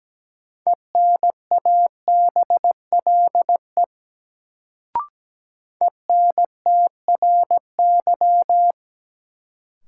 Morse code, heard as a keyed tone, about 17 wpm, 700 Hz. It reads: ENABLE ENTRY
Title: Sentences from Top 1500 Words - New Word Drill - 17wpm